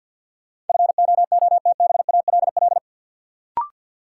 Morse code, heard as a keyed tone, about 53 wpm, 700 Hz. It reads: FOOTBALL
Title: Sentences from Top 1600 Words - New Word Drill Alt - 50wpm